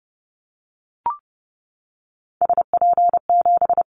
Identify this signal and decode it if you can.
Morse code, heard as a keyed tone, about 30 wpm, 700 Hz. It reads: SP7
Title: Callsign Prefixes for Top 25 DXCC - 30wpm